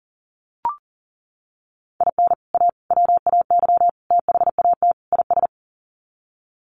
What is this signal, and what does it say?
Morse code, heard as a keyed tone, about 40 wpm, 700 Hz. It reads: IN A WAY THAT IS
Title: Sets of 5 Words Encore 40wpm